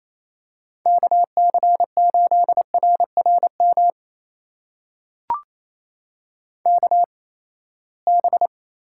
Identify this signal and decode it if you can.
Morse code, heard as a keyed tone, about 28 wpm, spaced as for 19 wpm, 700 Hz. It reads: KC8RRM KB
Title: USA Call Signs - 6 Characters - 8x Character Spacing - 28wpm